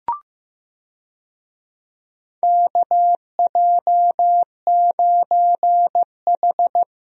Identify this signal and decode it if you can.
Morse code, heard as a keyed tone, about 16 wpm, 700 Hz. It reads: KJ9H
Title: USA Call Signs - 4 Characters - Repeated 2x - 15wpm